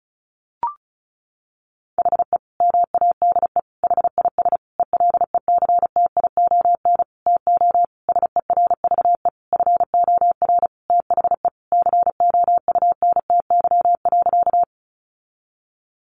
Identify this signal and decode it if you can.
Morse code, heard as a keyed tone, about 34 wpm, 700 Hz. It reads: HE MADE HIS ELECTION TO SERVE FOR THE COUNTY.
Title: Sentences from Top 1500 Words 35wpm